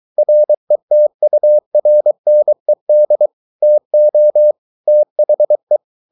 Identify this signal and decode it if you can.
Morse code, heard as a keyed tone, about 23 wpm, 600 Hz. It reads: RETURNED TO THE